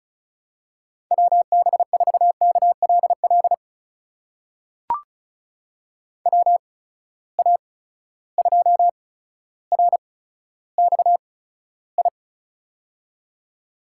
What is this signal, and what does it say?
Morse code, heard as a keyed tone, about 36 wpm, spaced as for 15 wpm, 700 Hz. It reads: WB4KLL WA2RXI
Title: USA Call Signs - 6 Characters - 8x Character Spacing - 35wpm